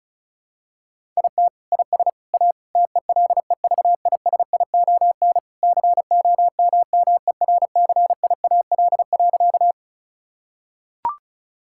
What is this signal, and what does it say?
Morse code, heard as a keyed tone, about 36 wpm, 700 Hz. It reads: IT IS A TELEVISION COMMERCIAL.
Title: Sentences from Top 1000 Words - Review Part 5 - 35wpm